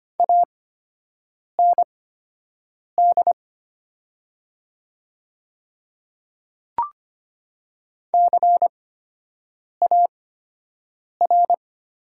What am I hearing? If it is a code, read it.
Morse code, heard as a keyed tone, about 25 wpm, spaced as for 7 wpm, 700 Hz. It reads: AND CAR
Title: 8x Character Spacing - Top 500 Words - 25wpm